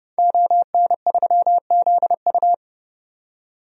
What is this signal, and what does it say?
Morse code, heard as a keyed tone, about 30 wpm, 700 Hz. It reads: ON3ZU
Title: DXCC-100 Call Signs - 5 Characters - 30wpm